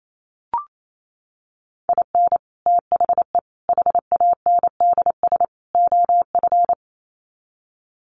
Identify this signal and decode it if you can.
Morse code, heard as a keyed tone, about 28 wpm, 700 Hz. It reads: IN THE HANDS OF
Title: Sets of 4 Words 28wpm